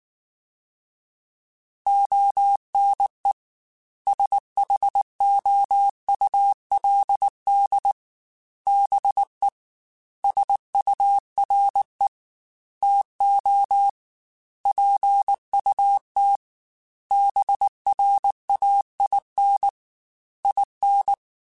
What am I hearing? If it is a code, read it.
Morse code, heard as a keyed tone, about 19 wpm, 800 Hz. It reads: ONE SHOULD BE SURE TO PUT BRAIN IN
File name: Chuckles / Chuckles-mp3s / BrainMouth @18